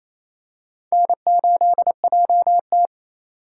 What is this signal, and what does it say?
Morse code, heard as a keyed tone, about 28 wpm, 700 Hz. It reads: N8JT
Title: USA Call Signs - 4 Characters - 4x Character Spacing - 28wpm